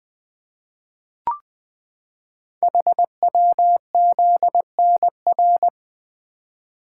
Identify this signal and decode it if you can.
Morse code, heard as a keyed tone, about 21 wpm, 700 Hz. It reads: HWZNR
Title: Code Groups - 5 Letter:Numbers - 20wpm